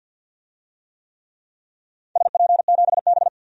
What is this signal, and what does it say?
Morse code, heard as a keyed tone, about 53 wpm, 700 Hz. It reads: SP7B
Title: DXCC-100 Call Signs - 4 Characters - 50wpm